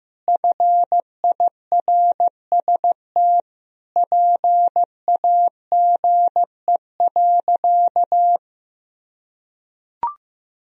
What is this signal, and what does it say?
Morse code, heard as a keyed tone, about 15 wpm, 700 Hz. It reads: FIRST PAGE.
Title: Sentences from Top 1000 Words - Review Part 1 - 15wpm